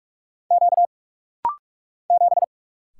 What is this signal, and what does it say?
Morse code, heard as a keyed tone, about 44 wpm, 700 Hz. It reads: Q 7
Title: Single Letter-Number - Rapid-Fire 45wpm